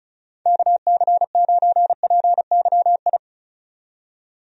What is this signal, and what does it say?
Morse code, heard as a keyed tone, about 34 wpm, 700 Hz. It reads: KC9PYI
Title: USA Call Signs - 6 Characters - 35wpm